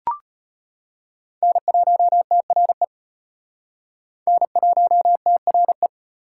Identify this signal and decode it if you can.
Morse code, heard as a keyed tone, about 32 wpm, 700 Hz. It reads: N1TRE N1TRE